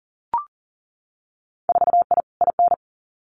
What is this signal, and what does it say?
Morse code, heard as a keyed tone, about 38 wpm, 700 Hz. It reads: 4I IN
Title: Winter Field Day Exchanges Encore 40wpm